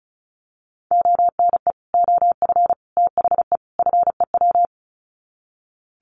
Morse code, heard as a keyed tone, about 34 wpm, 700 Hz. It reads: ONE OF THE FEW
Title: Sets of 4 Words 35wpm